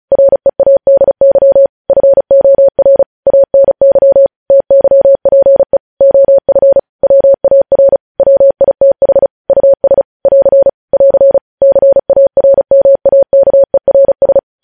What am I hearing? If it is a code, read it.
Morse code, heard as a keyed tone, about 36 wpm, 550 Hz. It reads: READY FOR ANY TYPE OF WAR WITH US <AR> <AR> CARMAKERS
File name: podcast / News headlines-1x-35x35